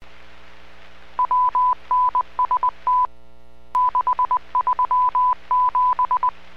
Morse code, heard as a keyed tone, about 20 wpm, spaced as for 17 wpm, 1000 Hz. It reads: WNST637
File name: WPS / 00203 10-52 WPS 451.25 WPS